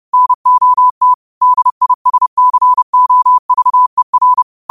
Morse code, heard as a keyed tone, about 30 wpm, 1000 Hz. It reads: NOT DISCOVER
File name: qotdfiles / 30wpm